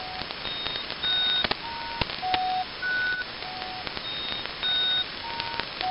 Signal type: Emergency Alert System (SAME) tones